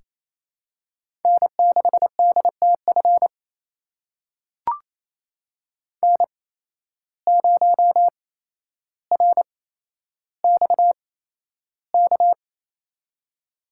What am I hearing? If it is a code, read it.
Morse code, heard as a keyed tone, about 28 wpm, spaced as for 12 wpm, 700 Hz. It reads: N6DTF N0RXK
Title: USA Call Signs - 8x Character Spacing - 28wpm